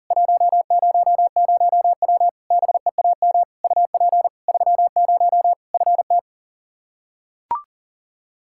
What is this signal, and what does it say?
Morse code, heard as a keyed tone, about 40 wpm, 700 Hz. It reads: 100W BEAM UP 30 FT
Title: Practice QSO - Exchange 2 - Info - 40wpm